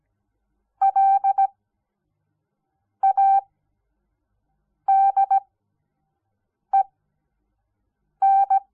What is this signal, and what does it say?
Morse code, heard as a keyed tone, about 17 wpm, spaced as for 5 wpm, 780 Hz. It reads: LADEN
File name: Onlinelehrgang / morsen / DJ4UF Morsen06